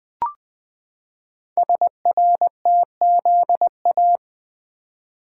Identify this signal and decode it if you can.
Morse code, heard as a keyed tone, about 20 wpm, 700 Hz. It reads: SRTZA
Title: Code Groups - 5 Letters - 20wpm